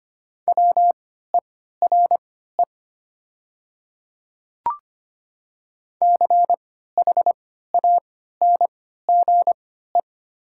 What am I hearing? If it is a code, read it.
Morse code, heard as a keyed tone, about 25 wpm, spaced as for 14 wpm, 700 Hz. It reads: WERE CHANGE
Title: 3x Character Spacing - Top 500 Words - 25wpm